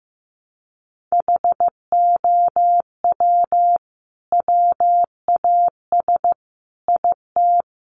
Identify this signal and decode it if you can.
Morse code, heard as a keyed tone, about 15 wpm, 700 Hz. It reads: HOW WAS IT
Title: Sentences from Top 500 Words 15wpm